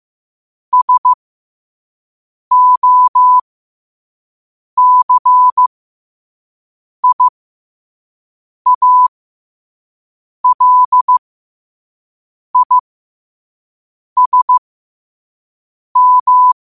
Morse code, heard as a keyed tone, about 15 wpm, spaced as for 5 wpm, 1000 Hz. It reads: SOCIALISM
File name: qotdfiles / 5wpm